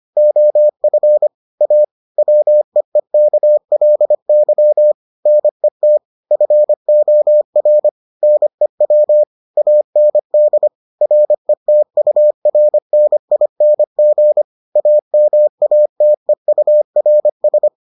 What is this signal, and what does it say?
Morse code, heard as a keyed tone, about 25 wpm, 600 Hz. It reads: OF A WEEKLY NET FOR NEW AND RETURNING AMATEURS